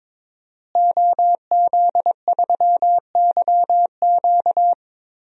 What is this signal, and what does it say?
Morse code, heard as a keyed tone, about 22 wpm, 700 Hz. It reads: OZ3YQ
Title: DXCC-100 Call Signs - 5 Characters - Repeated 2x - 22wpm